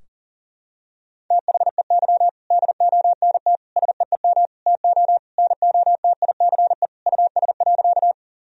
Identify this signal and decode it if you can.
Morse code, heard as a keyed tone, about 40 wpm, 700 Hz. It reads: THEY DONT SEEM TO NOTICE US.
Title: Sentences from Top 1000 Words - Review Part 2 - 40wpm